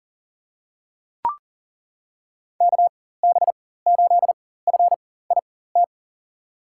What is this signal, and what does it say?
Morse code, heard as a keyed tone, about 40 wpm, spaced as for 19 wpm, 700 Hz. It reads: KB8FIT
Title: USA Call Signs - 4x Character Spacing - 40wpm